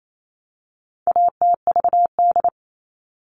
Morse code, heard as a keyed tone, about 28 wpm, 700 Hz. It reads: ATVD
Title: All Cut Numbers - 4-Digit 28wpm